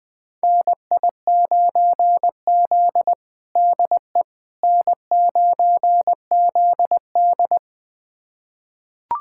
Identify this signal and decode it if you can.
Morse code, heard as a keyed tone, about 21 wpm, 700 Hz. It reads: NI9Z DE N9ZD
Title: USA Callsign DE Callsign - 4 Characters - 20wpm